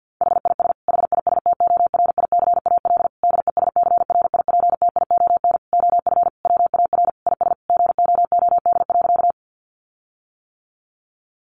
Morse code, heard as a keyed tone, about 50 wpm, 700 Hz. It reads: HIS HISTORICAL DESCRIPTION OF WAR IS GOOD.
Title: Sentences from Top 1300 Words 50wpm